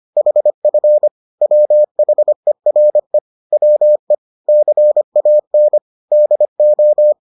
Morse code, heard as a keyed tone, about 25 wpm, 600 Hz. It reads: HF WHERE WE CAN DO